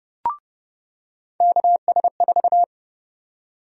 Morse code, heard as a keyed tone, about 31 wpm, 700 Hz. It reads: KS4
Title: USA Call Sign Prefixes - 30wpm